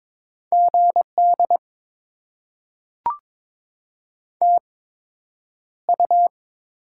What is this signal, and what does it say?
Morse code, heard as a keyed tone, about 22 wpm, spaced as for 9 wpm, 700 Hz. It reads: GD TU
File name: Top 90 QSO Elements - 8x Character Spacing - 22wpm